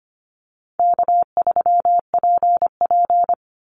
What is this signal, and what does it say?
Morse code, heard as a keyed tone, about 25 wpm, 700 Hz. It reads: K3PP
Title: Easy Call Signs - 25wpm